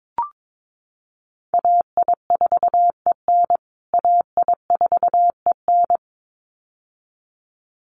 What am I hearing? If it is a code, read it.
Morse code, heard as a keyed tone, about 22 wpm, 700 Hz. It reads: AI4EN AI4EN
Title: USA Call Signs Repeated 2x 22wpm